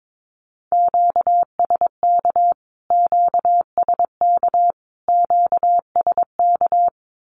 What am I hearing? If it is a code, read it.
Morse code, heard as a keyed tone, about 22 wpm, 700 Hz. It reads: QSK QSK QSK
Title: Q-Codes Repeated 3x 22wpm